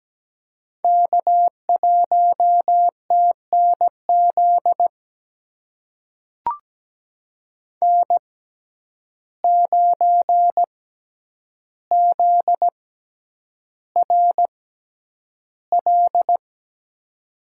Morse code, heard as a keyed tone, about 17 wpm, spaced as for 9 wpm, 700 Hz. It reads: K1TNZ N9ZRL
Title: USA Call Signs - 5 Characters - 6x Character Spacing - 17wpm